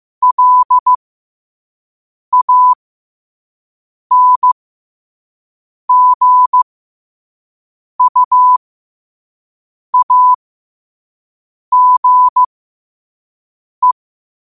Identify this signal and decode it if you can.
Morse code, heard as a keyed tone, about 15 wpm, spaced as for 5 wpm, 1000 Hz. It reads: LANGUAGE
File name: qotdfiles / 5wpm